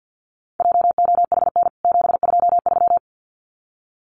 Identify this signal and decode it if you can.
Morse code, heard as a keyed tone, about 49 wpm, 700 Hz. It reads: JOHN 723
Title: CWT Exchange 50wpm